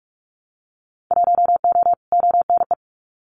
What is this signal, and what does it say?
Morse code, heard as a keyed tone, about 44 wpm, 700 Hz. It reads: 1O ONE
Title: Winter Field Day Exchanges 45wpm